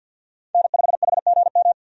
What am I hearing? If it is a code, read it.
Morse code, heard as a keyed tone, about 51 wpm, 700 Hz. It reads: N5HGM
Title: USA Call Signs - 5 Characters - 50wpm